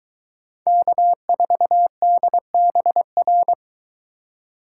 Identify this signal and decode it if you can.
Morse code, heard as a keyed tone, about 23 wpm, 700 Hz. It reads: K4DBR